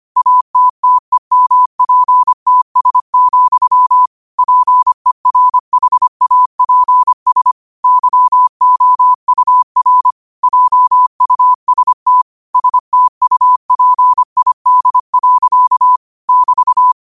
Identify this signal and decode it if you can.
Morse code, heard as a keyed tone, about 25 wpm, 1000 Hz. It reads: ATTEMPTS, PERHAPS YOUR JUST STUPID. <BT>
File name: qotdfiles / 25wpm